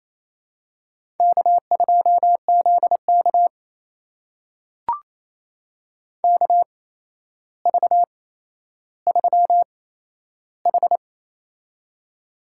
Morse code, heard as a keyed tone, about 28 wpm, spaced as for 12 wpm, 700 Hz. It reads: K2ZK KV3H
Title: USA Call Signs - 4 Characters - 8x Character Spacing - 28wpm